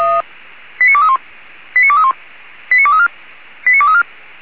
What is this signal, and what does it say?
Five-tone selective call (ZVEI), heard as quick runs of five tones, tones of about 80 ms. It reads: 89231 89231 89234 89234